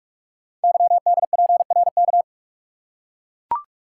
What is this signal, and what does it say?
Morse code, heard as a keyed tone, about 44 wpm, 700 Hz. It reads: YDPRK
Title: Code Groups - 5 Letters - 45wpm